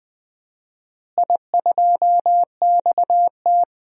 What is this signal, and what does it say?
Morse code, heard as a keyed tone, about 20 wpm, 700 Hz. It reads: I2XT
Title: DXCC-100 Call Signs - 4 Characters - Repeated 2x - 20wpm